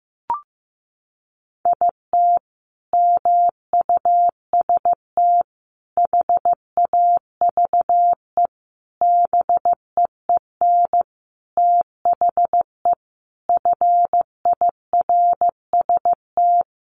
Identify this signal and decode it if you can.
Morse code, heard as a keyed tone, about 15 wpm, 700 Hz. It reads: IT MUST HAVE BEEN THE FIRST
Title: Sentences from Top 800 Words 15wpm